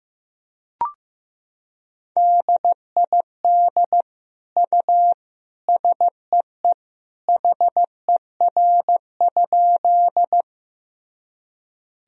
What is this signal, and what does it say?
Morse code, heard as a keyed tone, about 15 wpm, 700 Hz. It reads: DID U SEE HER?
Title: Sentences from Top 100 Words and Abbreviations 15wpm